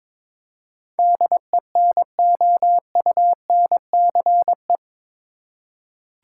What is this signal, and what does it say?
Morse code, heard as a keyed tone, about 22 wpm, 700 Hz. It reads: DENOUNCE
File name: practice / 22wpm / Eight Letter Words - 22wpm